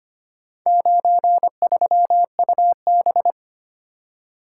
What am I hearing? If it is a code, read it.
Morse code, heard as a keyed tone, about 25 wpm, 700 Hz. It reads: 93UB
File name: practice / 25wpm / Four Letter-Number Combinations - 25wpm